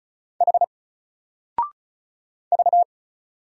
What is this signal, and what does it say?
Morse code, heard as a keyed tone, about 36 wpm, 700 Hz. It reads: H V
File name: Single-Letter-Number-35wpm